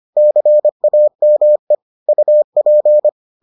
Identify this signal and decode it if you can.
Morse code, heard as a keyed tone, about 25 wpm, 600 Hz. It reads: CAME UP